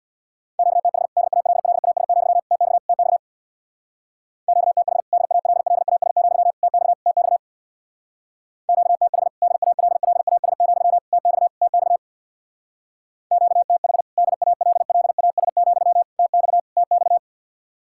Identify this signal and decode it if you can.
Morse code, heard as a keyed tone, about 62 wpm, 700 Hz. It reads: QTH DALLAS, TX TX QTH DALLAS, TX TX QTH DALLAS, TX TX QTH DALLAS, TX TX
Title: Speed-Racing Common QSO Phrases - 50wpm